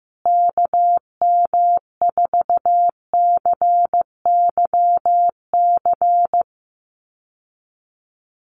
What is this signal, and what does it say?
Morse code, heard as a keyed tone, about 15 wpm, 700 Hz. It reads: KM4CYC
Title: USA Call Signs Repeated 3x 15wpm